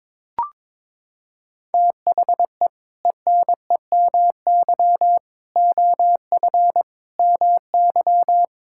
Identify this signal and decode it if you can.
Morse code, heard as a keyed tone, about 21 wpm, 700 Hz. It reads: THE ENEMY OF MY
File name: practice / 22wpm / Sentences with 8-Word Phrases - 22wpm